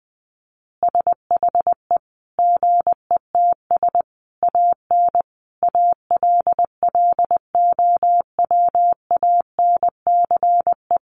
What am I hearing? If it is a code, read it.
Morse code, heard as a keyed tone, about 20 wpm, 700 Hz. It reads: SHE GETS AN ALLOWANCE
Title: Easy Phrases 20wpm